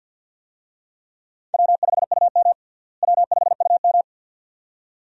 Morse code, heard as a keyed tone, about 45 wpm, 700 Hz. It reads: W5UM W5UM
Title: USA Call Signs - 4 Characters - Repeated 2x - 50wpm